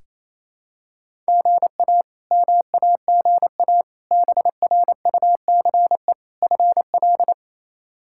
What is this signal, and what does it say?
Morse code, heard as a keyed tone, about 28 wpm, 700 Hz. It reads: GA MAGA BRUCE FL
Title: K1USN SST QSO Practice - 28wpm